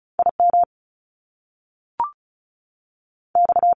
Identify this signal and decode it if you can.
Morse code, heard as a keyed tone, about 36 wpm, spaced as for 6 wpm, 700 Hz. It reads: IMX
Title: Top Two Letter Combinations in Words 35wpm